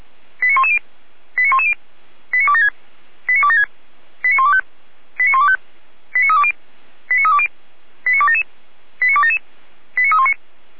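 Five-tone selective call (ZVEI), heard as quick runs of five tones, tones of about 70 ms. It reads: 89110 89110 89276 89276 89125 89125 89320 89320 89277 89277 89319
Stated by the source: ZVEI five-tone call 89050 received on an SDR.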